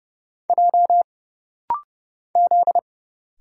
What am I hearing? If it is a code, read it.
Morse code, heard as a keyed tone, about 30 wpm, 700 Hz. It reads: J Z
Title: Single Letters - Rapid-Fire 30wpm